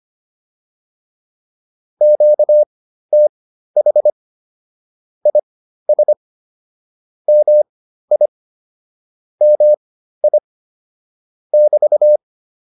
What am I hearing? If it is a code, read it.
Morse code, heard as a keyed tone, about 25 wpm, spaced as for 13 wpm, 600 Hz. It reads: QTH IS MI MI <BT>
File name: QSO 108 13